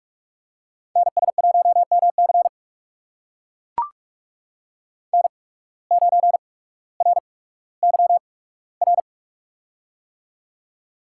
Morse code, heard as a keyed tone, about 43 wpm, spaced as for 19 wpm, 700 Hz. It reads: NS1MC N9RYR
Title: USA Call Signs - 8x Character Spacing - 45wpm